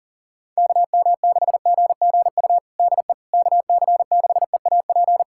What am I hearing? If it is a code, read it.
Morse code, heard as a keyed tone, about 40 wpm, 700 Hz. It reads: KM6GGU DE KC6EAP
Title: USA Callsign DE Callsign 40wpm